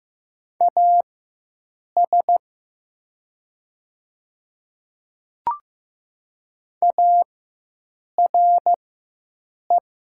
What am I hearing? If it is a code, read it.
Morse code, heard as a keyed tone, about 16 wpm, spaced as for 7 wpm, 700 Hz. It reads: AS ARE
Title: 4x Character Spacing - Top 100 Words - 15wpm